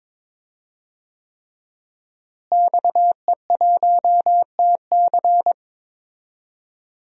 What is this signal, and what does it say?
Morse code, heard as a keyed tone, about 22 wpm, 700 Hz. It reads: XE1TC
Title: DXCC-100 Call Signs - 5 Characters - Repeated 3x - 22wpm